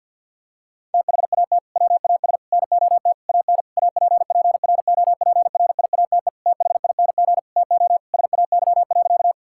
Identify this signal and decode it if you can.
Morse code, heard as a keyed tone, about 50 wpm, 700 Hz. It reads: THAT WAS NOT AN APPROPRIATE THING TO SAY.
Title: Sentences from Top 1000 Words - Review Part 5 - 50wpm